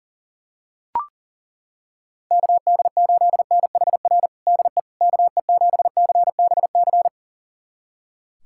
Morse code, heard as a keyed tone, about 40 wpm, 700 Hz. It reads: KD8NHR DE KE7CBC
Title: USA Callsign DE Callsign 40wpm